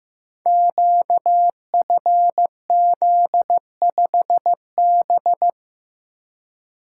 Morse code, heard as a keyed tone, about 15 wpm, 700 Hz. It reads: QFZ5B
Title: Code Groups - 5 Letter:Numbers - 15wpm